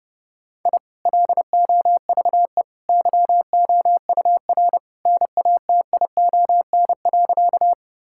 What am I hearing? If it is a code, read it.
Morse code, heard as a keyed tone, about 32 wpm, 700 Hz. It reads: I LOVE YOUR NATION.
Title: Sentences from Top 1000 Words - Review Part 6 - 30wpm